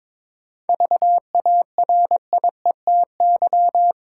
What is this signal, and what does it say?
Morse code, heard as a keyed tone, about 22 wpm, 700 Hz. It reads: VARIETY
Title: Sentences from Top 1000 Words - Review Part 4 - New Word Drill - 22wpm